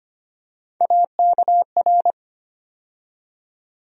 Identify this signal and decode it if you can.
Morse code, heard as a keyed tone, about 25 wpm, 700 Hz. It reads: AKR
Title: USA Call Sign Postfixes - 3 Letters 25wpm